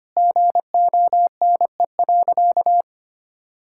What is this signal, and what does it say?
Morse code, heard as a keyed tone, about 25 wpm, 700 Hz. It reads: GONE.
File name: practice / 25wpm / Sentences with 3-Word Phrases - 25wpm